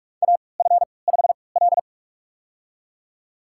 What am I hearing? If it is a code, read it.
Morse code, heard as a keyed tone, about 44 wpm, spaced as for 25 wpm, 700 Hz. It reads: AF5L
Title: USA Call Signs - 3x Character Spacing - 45wpm